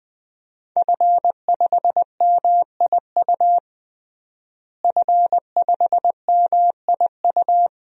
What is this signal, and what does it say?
Morse code, heard as a keyed tone, about 20 wpm, 700 Hz. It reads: F5MIU F5MIU
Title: DXCC-100 Call Signs - 5 Characters - Repeated 2x - 20wpm